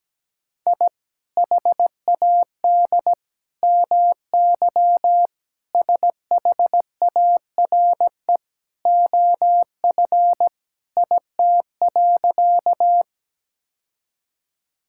Morse code, heard as a keyed tone, about 17 wpm, 700 Hz. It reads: I HAD MY SHARE OF IT.
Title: Sentences from Top 1000 Words - Review Part 6 - 17wpm